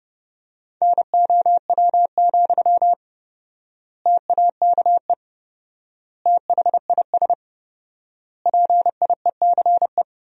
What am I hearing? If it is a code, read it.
Morse code, heard as a keyed tone, about 30 wpm, 700 Hz. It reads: NOW, TAKE THIS PIECE
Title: 4x Word-Spacing - Sentences from Top 400 Words - 30wpm